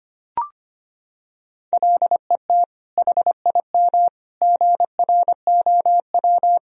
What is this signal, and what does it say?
Morse code, heard as a keyed tone, about 25 wpm, 700 Hz. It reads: LET HIM GROW
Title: Sentences from Top 300 Words and Abbreviations 25wpm